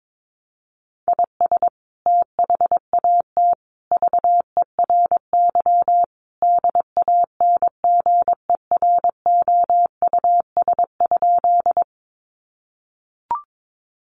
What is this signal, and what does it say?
Morse code, heard as a keyed tone, about 22 wpm, 700 Hz. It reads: IS THAT VERY DANGEROUS?
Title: Sentences from Top 1200 Words 22wpm